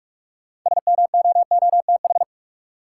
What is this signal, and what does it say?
Morse code, heard as a keyed tone, about 42 wpm, 700 Hz. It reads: SMOOTH
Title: Sentences from Top 1400 Words - New Word Drill - 45wpm